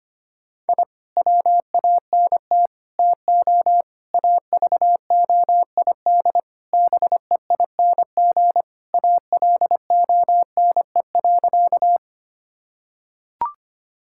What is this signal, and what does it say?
Morse code, heard as a keyed tone, about 25 wpm, 700 Hz. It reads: I WANT TO AVOID BEING ALONE.
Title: Sentences from Top 1000 Words - Review Part 6 - 25wpm